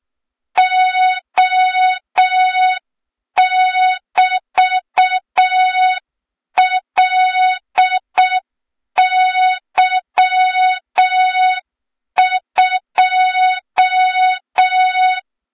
Morse code, heard as a keyed tone, about 6 wpm, 3700 Hz, with noise. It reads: O<BT>LY2
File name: Morze-HA2MN / hang / 23 GYAKORLAT